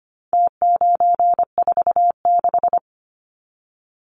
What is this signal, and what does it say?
Morse code, heard as a keyed tone, about 25 wpm, 700 Hz. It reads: T946
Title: Numbers - Contest Style - 4-Digit 25wpm